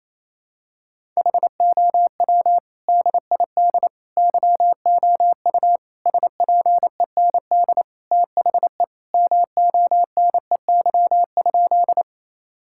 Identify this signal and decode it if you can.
Morse code, heard as a keyed tone, about 28 wpm, 700 Hz. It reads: HOW DID YOU SPEND THE MONEY?
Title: Sentences from Top 1000 Words - Review Part 5 - 28wpm